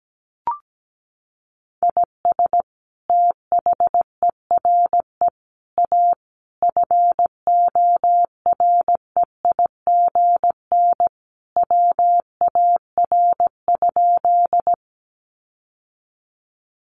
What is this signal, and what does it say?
Morse code, heard as a keyed tone, about 17 wpm, 700 Hz. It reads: IS THERE A FOREIGN WAR?
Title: Sentences from Top 1400 Words 17wpm